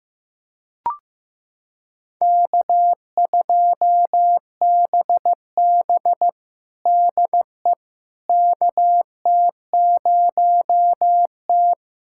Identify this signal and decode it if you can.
Morse code, heard as a keyed tone, about 15 wpm, 700 Hz. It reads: K2BB DE KT0T
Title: USA Callsign DE Callsign - 4 Characters - 15wpm